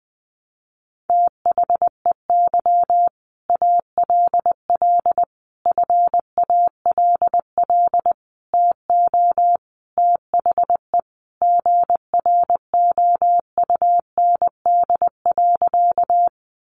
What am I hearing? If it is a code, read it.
Morse code, heard as a keyed tone, about 20 wpm, 700 Hz. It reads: THEY ALL FALL TO THE GROUND.
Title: Sentences from Top 500 Words 20wpm